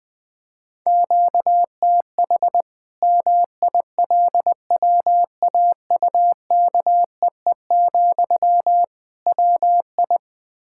Morse code, heard as a keyed tone, about 20 wpm, 700 Hz. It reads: QTH MILWAUKEE, WI
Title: Practice QSO - Exchange 1 - Beginning - 20wpm